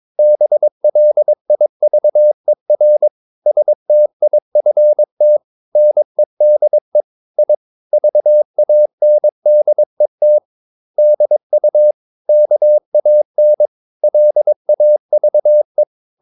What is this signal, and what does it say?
Morse code, heard as a keyed tone, about 22 wpm, 600 Hz. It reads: BLIVER STIFT NEDE I VANDET DU KAN LAVE
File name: Ferie QRS 22wpm